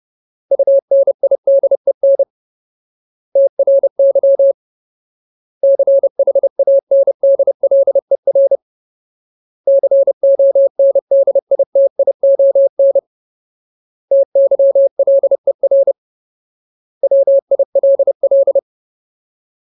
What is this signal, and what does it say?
Morse code, heard as a keyed tone, about 30 wpm, 550 Hz. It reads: UNIDEN TRY CHANDLER CONDITION TYLER WILL
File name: Ultimate Word List / 30WPM Ultimate